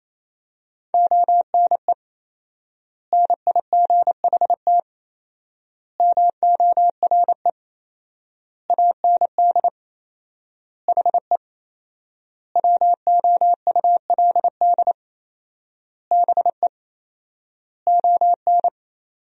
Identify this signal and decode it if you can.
Morse code, heard as a keyed tone, about 28 wpm, 700 Hz. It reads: ONE NIGHT MORE AND HE WOULD BE ON
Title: 4x Word-Spacing - Sentences from Top 300 Words - 28wpm